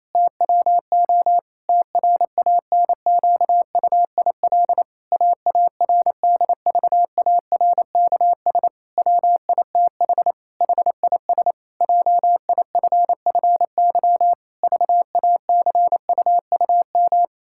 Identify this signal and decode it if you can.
Morse code, heard as a keyed tone, about 28 wpm, 700 Hz. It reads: TWO TRANQUIL AARDVARKS WITH HIS JIFFY VACUUM
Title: Pangram Challenge v2 - 28wpm